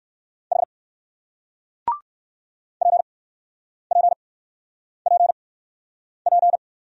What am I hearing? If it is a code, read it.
Morse code, heard as a keyed tone, about 54 wpm, spaced as for 9 wpm, 700 Hz. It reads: H PPPP
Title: Speed-Racing Single Letter-Number 45wpm